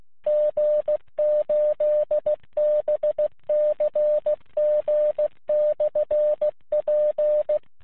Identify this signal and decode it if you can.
Morse code, heard as a keyed tone, about 16 wpm, 600 Hz. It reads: G8BCG/P
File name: jh5fis-qso